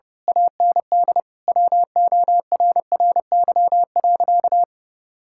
Morse code, heard as a keyed tone, about 30 wpm, 700 Hz. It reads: AND WORRY.
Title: Pangram Challenge v2 - 30wpm